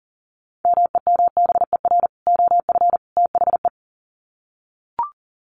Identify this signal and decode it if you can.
Morse code, heard as a keyed tone, about 40 wpm, 700 Hz. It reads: MEMBER OF THE
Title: Sets of 3 Words 40wpm